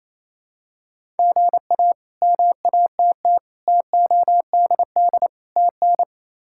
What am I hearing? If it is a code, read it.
Morse code, heard as a keyed tone, about 27 wpm, 700 Hz. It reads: GA MATT TODD TN
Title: K1USN SST QSO Practice - 28wpm